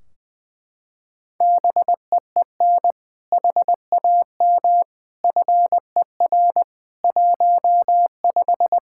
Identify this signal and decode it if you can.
Morse code, heard as a keyed tone, about 20 wpm, 700 Hz. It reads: BEEN HAM FER 15
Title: Practice QSO - Exchange 3 - More Info - 20wpm